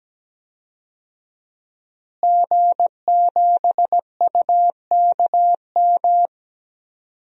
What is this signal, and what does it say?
Morse code, heard as a keyed tone, about 17 wpm, 700 Hz. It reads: G7UKM
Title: DXCC-100 Call Signs - 5 Characters - Repeated 2x - 17wpm